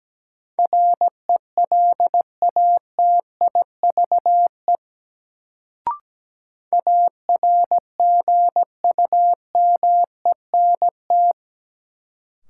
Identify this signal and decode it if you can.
Morse code, heard as a keyed tone, about 17 wpm, 700 Hz. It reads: RELATIVE ARGUMENT
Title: Sentences from Top 1400 Words - New Word Drill - 17wpm